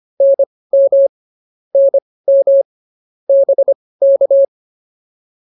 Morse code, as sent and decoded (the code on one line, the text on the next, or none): -. -- / -. -- / -... -.-
NM NM BK